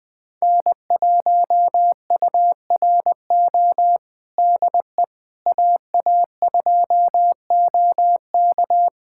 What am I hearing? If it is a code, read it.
Morse code, heard as a keyed tone, about 20 wpm, 700 Hz. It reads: N1URO DE AA2OK
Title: USA Callsign DE Callsign - 5 Characters - 20wpm